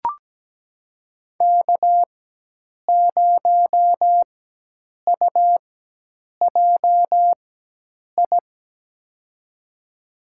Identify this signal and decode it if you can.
Morse code, heard as a keyed tone, about 17 wpm, spaced as for 8 wpm, 700 Hz. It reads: K0UJI
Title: USA Call Signs - 5 Characters - 4x Character Spacing - 17wpm